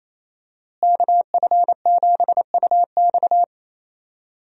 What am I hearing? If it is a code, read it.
Morse code, heard as a keyed tone, about 28 wpm, 700 Hz. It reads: KF7UX
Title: USA Call Signs - 5 Characters - Repeated 3x - 28wpm